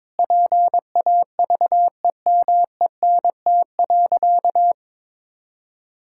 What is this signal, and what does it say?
Morse code, heard as a keyed tone, about 22 wpm, 700 Hz. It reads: PAVEMENT.
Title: Pangram Challenge v2 - 22wpm